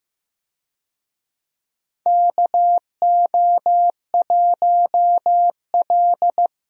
Morse code, heard as a keyed tone, about 15 wpm, 700 Hz. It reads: KO1L
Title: USA Call Signs - 4 Characters - 15wpm